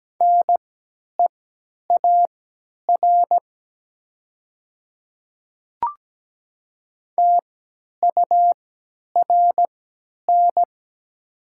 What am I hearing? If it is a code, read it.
Morse code, heard as a keyed tone, about 17 wpm, spaced as for 10 wpm, 700 Hz. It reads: NEAR TURN
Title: 3x Character Spacing - Top 200 Words - 17wpm